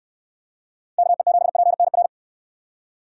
Morse code, heard as a keyed tone, about 63 wpm, 700 Hz. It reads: KE7PNR